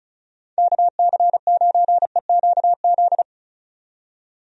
Morse code, heard as a keyed tone, about 35 wpm, 700 Hz. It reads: KC9EQZ
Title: USA Call Signs - 6 Characters - 35wpm